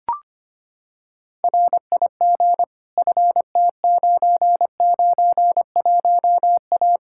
Morse code, heard as a keyed tone, about 25 wpm, 700 Hz. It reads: RIG FT991A
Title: Practice QSO - Exchange 2 - Info - 25wpm